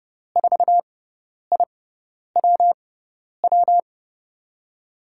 Morse code, heard as a keyed tone, about 30 wpm, spaced as for 10 wpm, 700 Hz. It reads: 4IWW
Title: USA Call Signs - 5 Characters - 6x Character Spacing - 30wpm